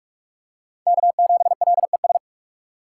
Morse code, heard as a keyed tone, about 46 wpm, 700 Hz. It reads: K7LES
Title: USA Call Signs - 3x Character Spacing - 45wpm